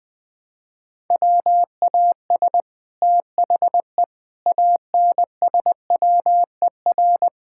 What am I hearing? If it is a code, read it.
Morse code, heard as a keyed tone, about 20 wpm, 700 Hz. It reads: WAS THE ANSWER
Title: Sentences from Top 1000 Words - Review Part 6 - 20wpm